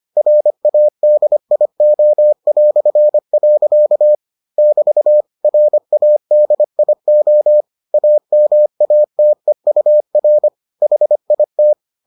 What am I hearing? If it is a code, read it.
Morse code, heard as a keyed tone, about 25 wpm, 600 Hz. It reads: RADIO". <BT> RADIO AMATEUR HIT